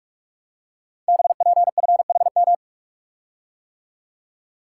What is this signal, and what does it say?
Morse code, heard as a keyed tone, about 44 wpm, 700 Hz. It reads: BPFHM
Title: Code Groups - 5 Letters - 45wpm